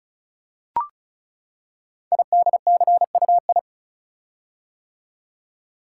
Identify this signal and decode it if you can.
Morse code, heard as a keyed tone, about 36 wpm, 700 Hz. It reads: IDCUI
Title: Code Groups - 5 Letter:Numbers - 35wpm